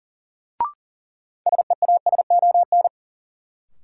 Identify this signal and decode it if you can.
Morse code, heard as a keyed tone, about 37 wpm, 700 Hz. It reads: SEASON
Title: Sets of 5 Words - New Word Drill - 40wpm